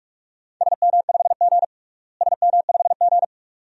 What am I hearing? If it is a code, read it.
Morse code, heard as a keyed tone, about 46 wpm, 700 Hz. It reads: SM5G SM5G
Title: DXCC-100 Call Signs - 4 Characters - Repeated 2x - 45wpm